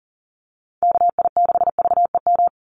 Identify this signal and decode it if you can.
Morse code, heard as a keyed tone, about 40 wpm, 700 Hz. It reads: KI6VEM
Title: USA Call Signs 40wpm